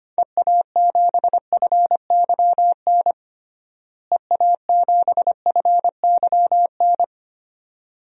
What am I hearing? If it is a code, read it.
Morse code, heard as a keyed tone, about 25 wpm, 700 Hz. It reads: EA7FYN EA7FYN
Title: DXCC-100 Call Signs - 6 Characters - Repeated 2x - 25wpm